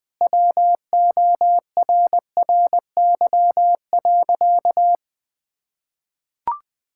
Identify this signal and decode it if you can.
Morse code, heard as a keyed tone, about 21 wpm, 700 Hz. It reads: WORRY.
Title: Sentences with 2-Word Phrases - 20wpm